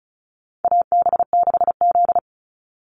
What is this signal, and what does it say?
Morse code, heard as a keyed tone, about 34 wpm, 700 Hz. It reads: AB6Z
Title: Call Signs Repeated 3x - 35wpm